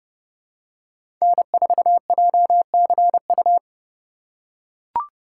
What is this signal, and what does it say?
Morse code, heard as a keyed tone, about 32 wpm, 700 Hz. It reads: N4JCU
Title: USA Call Signs - 5 Characters - 3x Character Spacing - 30wpm